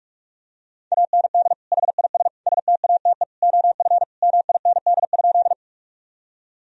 Morse code, heard as a keyed tone, about 46 wpm, 700 Hz. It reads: AND HIS STATE OF MIND?
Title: Sentences from Top 1000 Words - Review Part 2 - 45wpm